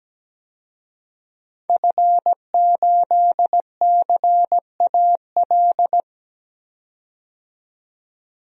Morse code, heard as a keyed tone, about 17 wpm, 700 Hz. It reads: F8CAL
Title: DXCC-100 Call Signs - 5 Characters - 17wpm